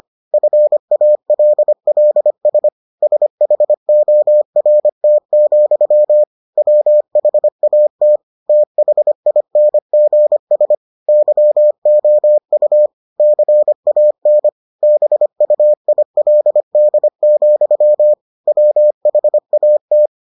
Morse code, heard as a keyed tone, about 25 wpm, 600 Hz. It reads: FALLS SHORT, WHAT THINGS YOU CAN BUILD, WHAT